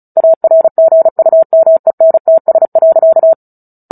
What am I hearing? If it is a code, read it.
Morse code, heard as a keyed tone, about 34 wpm, 650 Hz. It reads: ARGUMENTS.
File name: Book Two / 09-Chapter 9 - 35 WPM